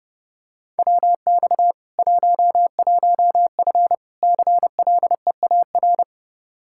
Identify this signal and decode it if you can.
Morse code, heard as a keyed tone, about 32 wpm, 700 Hz. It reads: WX 11F CLEAR
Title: Practice QSO - Exchange 2 - Info - 30wpm